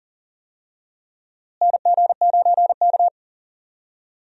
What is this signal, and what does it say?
Morse code, heard as a keyed tone, about 40 wpm, 700 Hz. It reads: NG9K
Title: USA Call Signs - 4 Characters - Repeated 2x - 40wpm